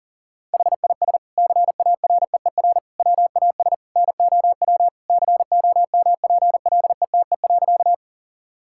Morse code, heard as a keyed tone, about 40 wpm, 700 Hz. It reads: HIS CAREER WAS NOW COMPLETE.
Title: Sentences from Top 1000 Words - Review Part 6 - 40wpm